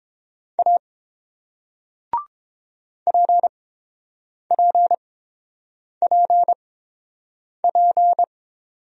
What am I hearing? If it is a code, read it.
Morse code, heard as a keyed tone, about 30 wpm, spaced as for 7 wpm, 690 Hz. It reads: A PPPP